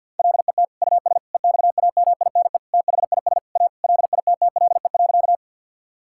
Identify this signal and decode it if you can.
Morse code, heard as a keyed tone, about 50 wpm, 700 Hz. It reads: LET US EXAMINE THIS A LITTLE.
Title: Sentences from Top 1600 Words 50wpm